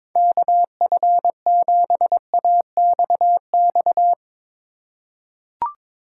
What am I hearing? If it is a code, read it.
Morse code, heard as a keyed tone, about 22 wpm, 700 Hz. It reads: KF7AXX
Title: USA Call Signs - 6 Characters - 2x Character Spacing - 22wpm